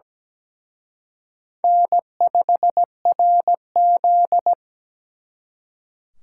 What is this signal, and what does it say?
Morse code, heard as a keyed tone, about 20 wpm, 700 Hz. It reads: N5RZ
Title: Speed Racing - USA 1x2 Call Signs 17wpm